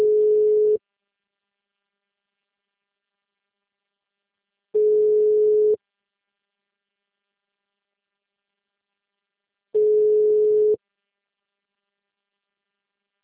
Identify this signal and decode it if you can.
Ringback tone (Europe, 425 Hz).
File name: out- -2003- - - .2493